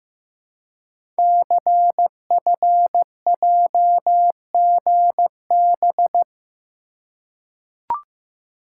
Morse code, heard as a keyed tone, about 15 wpm, 700 Hz. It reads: CFJGB
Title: Code Groups - 5 Letters - 15wpm